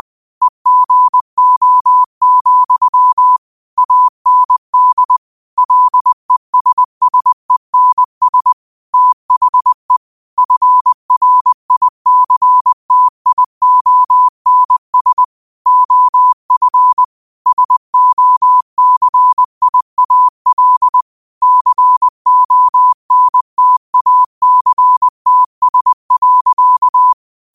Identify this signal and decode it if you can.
Morse code, heard as a keyed tone, about 20 wpm, 1000 Hz. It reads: EGO, AND LESSENS THE FRICTIONS OF SOCIAL CONTACTS.
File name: qotdfiles / 20wpm